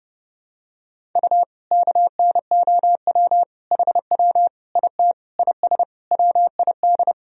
Morse code, heard as a keyed tone, about 30 wpm, 700 Hz. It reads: U KNOW HW IT IS WID
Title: Sentences from Top 100 Words and Abbreviations 30wpm